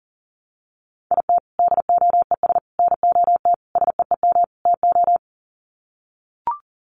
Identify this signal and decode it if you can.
Morse code, heard as a keyed tone, about 37 wpm, 700 Hz. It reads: IT DOES NOT SEEM TO
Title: Sets of 5 Words Encore2 40wpm